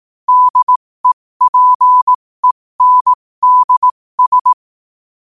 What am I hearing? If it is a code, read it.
Morse code, heard as a keyed tone, about 18 wpm, spaced as for 15 wpm, 1000 Hz. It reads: DEPENDS
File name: qotdfiles / 15wpm